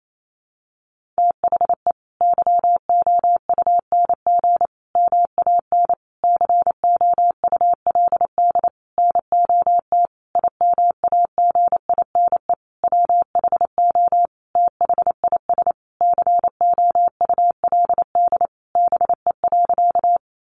Morse code, heard as a keyed tone, about 28 wpm, 700 Hz. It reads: THE YOUNG MAN COULD NOT IMAGINE WHO THIS COULD BE.
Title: Sentences from Top 1400 Words 28wpm